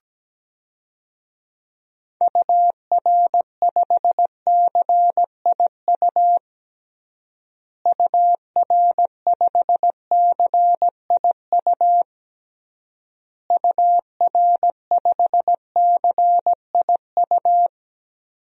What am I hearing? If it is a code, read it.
Morse code, heard as a keyed tone, about 17 wpm, 700 Hz. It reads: UR5CIU UR5CIU UR5CIU
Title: DXCC-100 Call Signs - 6 Characters - Repeated 3x - 17wpm